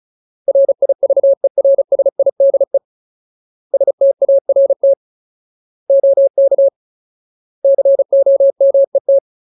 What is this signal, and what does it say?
Morse code, heard as a keyed tone, about 34 wpm, 550 Hz. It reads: RIVERSIDE START OK COMET
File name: Ultimate Word List / 35WPM Ultimate